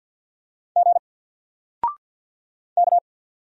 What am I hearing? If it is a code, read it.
Morse code, heard as a keyed tone, about 48 wpm, 700 Hz. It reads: G K
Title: Single Letter-Number Alt 50wpm